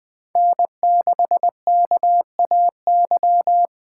Morse code, heard as a keyed tone, about 21 wpm, 700 Hz. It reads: N6KAY